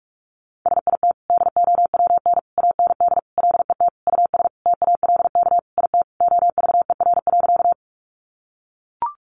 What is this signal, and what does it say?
Morse code, heard as a keyed tone, about 44 wpm, 700 Hz. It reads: SIT DOWN AND LET US TALK IT OVER.
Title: Sentences from Top 500 Words 45wpm